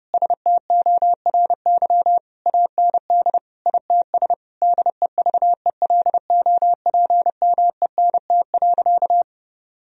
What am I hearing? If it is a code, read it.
Morse code, heard as a keyed tone, about 30 wpm, 700 Hz. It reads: STORY AND ITS DEVELOPMENT.
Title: Sentences from Top 1000 Words - Review Part 4 - 30wpm